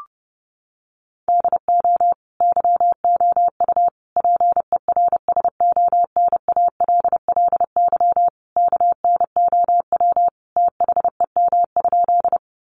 Morse code, heard as a keyed tone, about 30 wpm, 700 Hz. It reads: DO YOU PERSONALLY KNOW THEM?
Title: Sentences from Top 1400 Words 30wpm